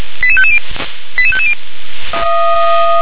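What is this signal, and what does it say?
Five-tone selective call (ZVEI), heard as quick runs of five tones, tones of about 70 ms. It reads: 88550 88550
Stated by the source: ZVEI five-tone selective call received on an SDR.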